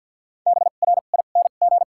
Morse code, heard as a keyed tone, about 54 wpm, 700 Hz. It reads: B R I N G
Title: 2x Character Spacing - Top 500 Words - 50wpm